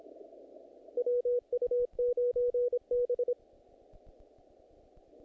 Morse code, heard as a keyed tone, about 22 wpm, 470 Hz. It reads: WU9B
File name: contests / naqp-cw-k5tr-op-k5pi- -1900